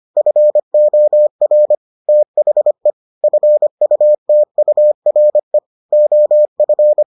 Morse code, heard as a keyed tone, about 25 wpm, 600 Hz. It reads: FOR THE FUTURE OF